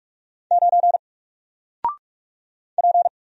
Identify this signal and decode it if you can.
Morse code, heard as a keyed tone, about 44 wpm, 700 Hz. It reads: 9 P
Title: Single Letter-Number Alt 45wpm